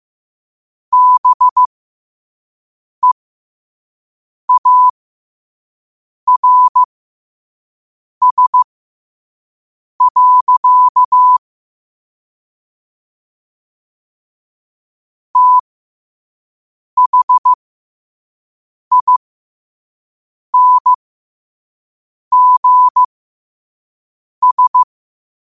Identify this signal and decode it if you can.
Morse code, heard as a keyed tone, about 15 wpm, spaced as for 5 wpm, 1000 Hz. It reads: BEARS. THINGS
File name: qotdfiles / 5wpm